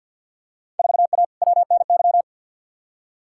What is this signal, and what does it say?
Morse code, heard as a keyed tone, about 48 wpm, 700 Hz. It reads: 4A WNY
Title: Field Day Exchanges Encore - 50wpm